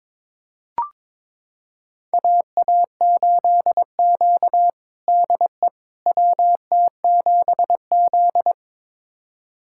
Morse code, heard as a keyed tone, about 22 wpm, 700 Hz. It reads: AA8Q DE WT7Z
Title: USA Callsign DE Callsign - 4 Characters - 22wpm